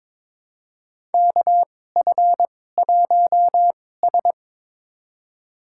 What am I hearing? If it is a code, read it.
Morse code, heard as a keyed tone, about 22 wpm, spaced as for 16 wpm, 700 Hz. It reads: KF1S
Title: USA Call Signs - 4 Characters - 2x Character Spacing - 22wpm